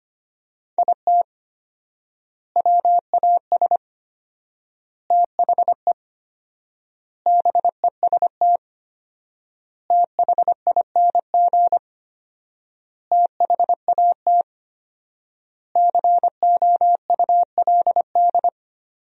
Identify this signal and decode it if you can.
Morse code, heard as a keyed tone, about 25 wpm, 700 Hz. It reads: IT WAS THE BEST THING THAT COULD
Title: 4x Word-Spacing - Sentences from Top 400 Words - 25wpm